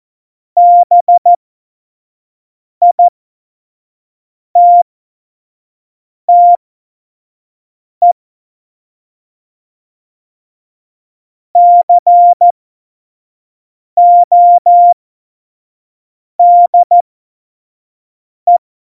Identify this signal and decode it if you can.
Morse code, heard as a keyed tone, about 14 wpm, spaced as for 5 wpm, 700 Hz. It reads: BITTE CODE
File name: Morsekurs / Witze / Witz 18 Juli 17, Tempo 25 F70